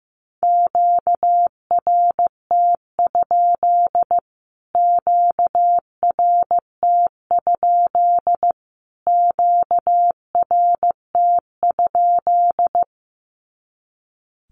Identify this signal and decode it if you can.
Morse code, heard as a keyed tone, about 15 wpm, 700 Hz. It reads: QRT? QRT? QRT?
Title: Q-Codes Repeated 3x 15wpm